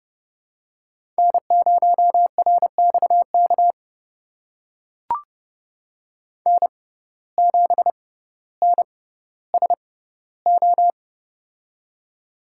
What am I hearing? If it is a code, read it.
Morse code, heard as a keyed tone, about 30 wpm, spaced as for 15 wpm, 700 Hz. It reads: N0RXK N7NSO
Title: USA Call Signs - 5 Characters - 6x Character Spacing - 30wpm